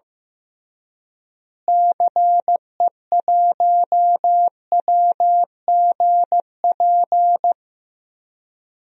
Morse code, heard as a keyed tone, about 15 wpm, 700 Hz. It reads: CE1WGP
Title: DXCC-100 Call Signs - 6 Characters - Repeated 3x - 15wpm